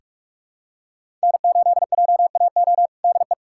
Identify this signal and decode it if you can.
Morse code, heard as a keyed tone, about 46 wpm, 700 Hz. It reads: N8JAO DE
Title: USA Callsign DE Callsign 45wpm